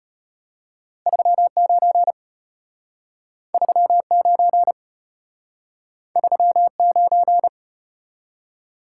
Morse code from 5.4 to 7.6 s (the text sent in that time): 39